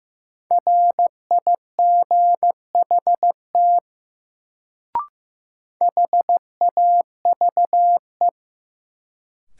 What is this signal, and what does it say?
Morse code, heard as a keyed tone, about 15 wpm, 700 Hz. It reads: RIGHT HAVE
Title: Sets of 2 Words - New Word Drill - 15wpm